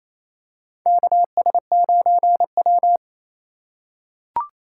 Morse code, heard as a keyed tone, about 28 wpm, 700 Hz. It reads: KS9W
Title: K1USN SST QSO Practice - 28wpm